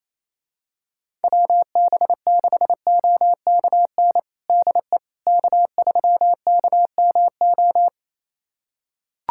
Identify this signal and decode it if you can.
Morse code, heard as a keyed tone, about 28 wpm, 700 Hz. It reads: WB6OKN DE K3KMO
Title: USA Callsign DE Callsign 28wpm